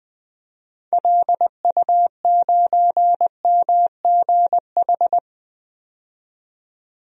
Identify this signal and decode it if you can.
Morse code, heard as a keyed tone, about 20 wpm, 700 Hz. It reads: LU9MGH
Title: DXCC-100 Call Signs - 6 Characters - 20wpm